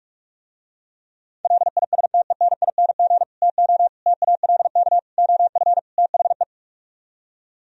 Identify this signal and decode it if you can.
Morse code, heard as a keyed tone, about 48 wpm, 700 Hz. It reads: LISTENING TO TALK OF THE